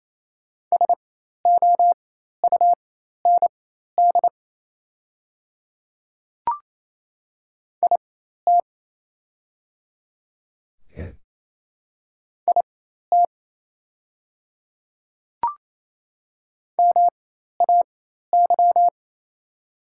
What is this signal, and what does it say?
Morse code, heard as a keyed tone, about 28 wpm, spaced as for 13 wpm, 690 Hz. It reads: SOUND IT IT MAY
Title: 4x Character Spacing - Top 100 Words - 28wpm